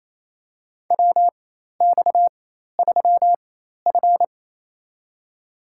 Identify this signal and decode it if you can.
Morse code, heard as a keyed tone, about 28 wpm, spaced as for 13 wpm, 700 Hz. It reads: WX3F
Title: USA Call Signs - 4 Characters - 4x Character Spacing - 28wpm